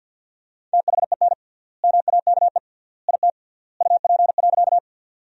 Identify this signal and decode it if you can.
Morse code, heard as a keyed tone, about 50 wpm, 700 Hz. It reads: THEN MAKE IT UP.
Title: 3x Word-Spacing - Sentences from Top 100 Words - 50wpm